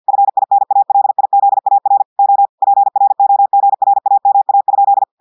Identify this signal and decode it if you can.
Morse code, heard as a keyed tone, about 50 wpm, 800 Hz. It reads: FINALIZAR O PROGRAMA?
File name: portugais texte en clair 2 vitesse 50 mots minutes